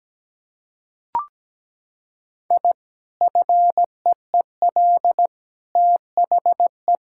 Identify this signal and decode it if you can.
Morse code, heard as a keyed tone, about 17 wpm, 700 Hz. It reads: I FEEL THE
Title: Sentences from Top 1000 Words - Review Part 6 - 17wpm